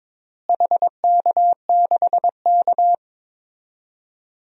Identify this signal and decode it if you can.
Morse code, heard as a keyed tone, about 22 wpm, 700 Hz. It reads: HK6K
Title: DXCC-100 Call Signs - 4 Characters - Repeated 2x - 22wpm